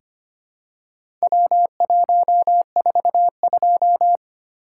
Morse code, heard as a keyed tone, about 25 wpm, 700 Hz. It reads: W142
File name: practice / 25wpm / Four Letter-Number Combinations - 25wpm